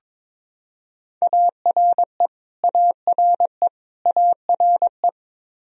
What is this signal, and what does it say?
Morse code, heard as a keyed tone, about 22 wpm, 700 Hz. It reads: ARE ARE ARE
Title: Top 100 Words Repeated 22wpm